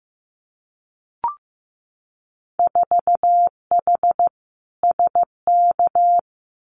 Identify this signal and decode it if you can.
Morse code, heard as a keyed tone, about 15 wpm, 700 Hz. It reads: 4H SK
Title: Winter Field Day Exchanges Encore 15wpm